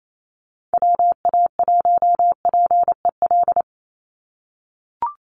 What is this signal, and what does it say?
Morse code, heard as a keyed tone, about 28 wpm, 700 Hz. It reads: WA1PEL
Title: USA Call Signs 28wpm